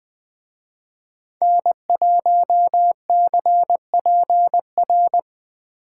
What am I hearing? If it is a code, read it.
Morse code, heard as a keyed tone, about 20 wpm, 700 Hz. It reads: N1CPR
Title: Speed Racing - USA 1x3 Call Signs 17wpm